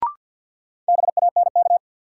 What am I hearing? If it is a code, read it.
Morse code, heard as a keyed tone, about 53 wpm, 700 Hz. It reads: BANK
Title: Sets of 5 Words - New Word Drill - 50wpm